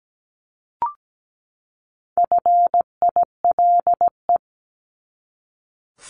Morse code, heard as a keyed tone, about 17 wpm, 700 Hz. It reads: FILE
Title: Top 1001-1250 Words 17wpm